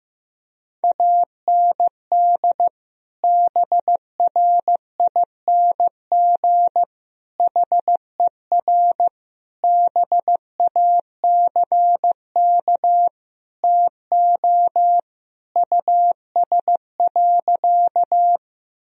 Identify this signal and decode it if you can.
Morse code, heard as a keyed tone, about 15 wpm, 700 Hz. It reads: AND BRING HER BACK TO US.
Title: Sentences from Top 1000 Words - Review Part 3 - 15wpm